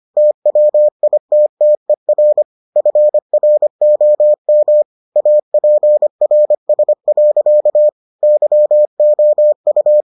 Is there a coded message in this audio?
Yes: Morse code from 0.2 to 10.0 s, heard as a keyed tone, about 25 wpm, 600 Hz. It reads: TWITTER FROM APRS. YOU